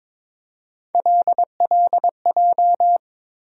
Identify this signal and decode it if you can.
Morse code, heard as a keyed tone, about 22 wpm, 700 Hz. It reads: LLJ
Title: USA Call Sign Postfixes - 3 Letters 22wpm